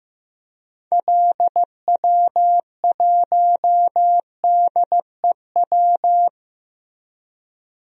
Morse code, heard as a keyed tone, about 16 wpm, 700 Hz. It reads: LW1DEW
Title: DXCC-100 Call Signs - 6 Characters - Repeated 2x - 15wpm